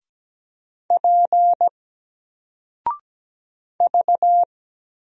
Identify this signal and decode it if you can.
Morse code, heard as a keyed tone, about 17 wpm, 700 Hz. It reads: P V
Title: Single Character (Letter:Number:Punctuation:Prosign) 17wpm